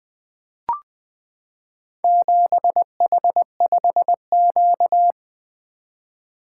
Morse code, heard as a keyed tone, about 20 wpm, 700 Hz. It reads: 7H5Q
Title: Four Letter-Number Combinations - 20wpm